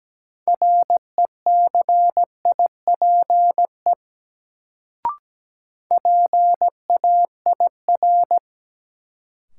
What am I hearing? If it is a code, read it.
Morse code, heard as a keyed tone, about 17 wpm, 700 Hz. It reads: RECIPE PAIR
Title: Sentences from Top 1200 Words - New Word Drill - 17wpm